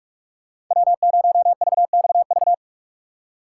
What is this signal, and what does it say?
Morse code, heard as a keyed tone, about 42 wpm, 700 Hz. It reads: W0VXV
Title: USA Call Signs - 5 Characters - 6x Character Spacing - 45wpm